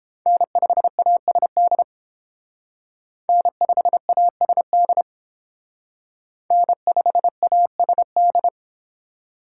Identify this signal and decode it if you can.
Morse code, heard as a keyed tone, about 30 wpm, 700 Hz. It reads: N5ASD N5ASD N5ASD